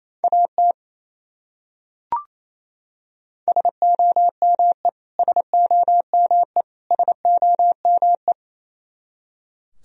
Morse code, heard as a keyed tone, about 28 wpm, 700 Hz. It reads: AT SOME SOME SOME
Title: Top 100 Words Repeated 28wpm